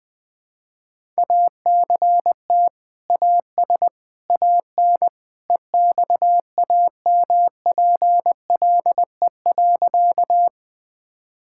Morse code, heard as a keyed tone, about 20 wpm, 700 Hz. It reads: ACT AS AN EXAMPLE.
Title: Sentences with 2-Word Phrases - 20wpm